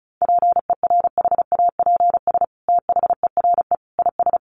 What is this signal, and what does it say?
Morse code, heard as a keyed tone, about 34 wpm, 700 Hz. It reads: PERHAPS THERE IS
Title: Sentences from Top 600 Words 35wpm